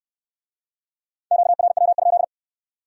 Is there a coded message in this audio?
Yes: Morse code from 1.3 to 2.2 s, heard as a keyed tone, about 64 wpm, 700 Hz. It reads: QRW?